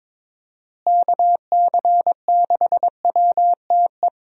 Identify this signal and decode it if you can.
Morse code, heard as a keyed tone, about 22 wpm, 700 Hz. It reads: KC6WTE
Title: USA Call Signs - 6 Characters - Repeated 2x - 22wpm